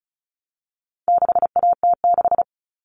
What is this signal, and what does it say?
Morse code, heard as a keyed tone, about 34 wpm, 700 Hz. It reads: 6AT6
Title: All Cut Numbers - 4-Digit 35wpm